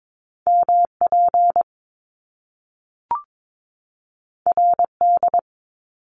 Morse code, heard as a keyed tone, about 22 wpm, 700 Hz. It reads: MP RD
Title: Top Two Letter Combinations in Words 22wpm